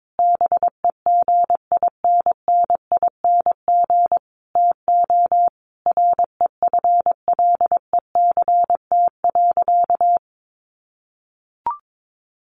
Morse code, heard as a keyed tone, about 22 wpm, 700 Hz. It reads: BEGINNING TO REFLECT.
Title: Sentences from Top 1500 Words 22wpm